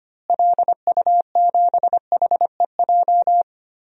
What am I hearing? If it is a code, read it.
Morse code, heard as a keyed tone, about 25 wpm, 700 Hz. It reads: LU7HEJ
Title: DXCC-100 Call Signs - 6 Characters - Repeated 3x - 25wpm